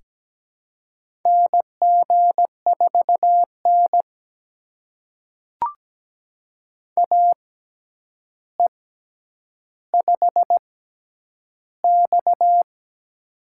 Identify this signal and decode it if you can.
Morse code, heard as a keyed tone, about 17 wpm, spaced as for 9 wpm, 700 Hz. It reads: NG4N AE5X
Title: USA Call Signs - 4 Characters - 6x Character Spacing - 17wpm